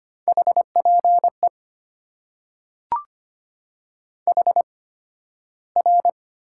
Morse code, heard as a keyed tone, about 25 wpm, spaced as for 13 wpm, 700 Hz. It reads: HPE HR
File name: Top 90 QSO Elements - 8x Character Spacing - 25wpm